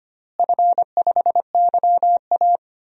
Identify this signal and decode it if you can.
Morse code, heard as a keyed tone, about 25 wpm, 700 Hz. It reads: F5YA
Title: DXCC-100 Call Signs - 4 Characters - Repeated 3x - 25wpm